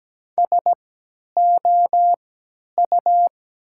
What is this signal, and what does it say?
Morse code, heard as a keyed tone, about 17 wpm, spaced as for 10 wpm, 700 Hz. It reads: SOU
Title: 3x Character Spacing - Top 200 Words - 17wpm